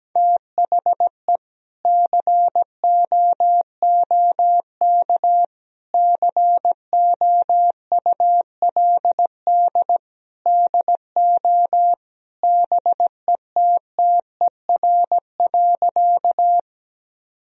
Morse code, heard as a keyed tone, about 17 wpm, 700 Hz. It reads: THE COOK COULD DO BETTER.
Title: Sentences from Top 1000 Words - Review Part 4 - 17wpm